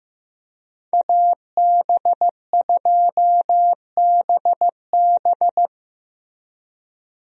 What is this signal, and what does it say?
Morse code, heard as a keyed tone, about 16 wpm, 700 Hz. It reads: AB2BB
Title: USA Call Signs - 5 Characters - Repeated 3x - 15wpm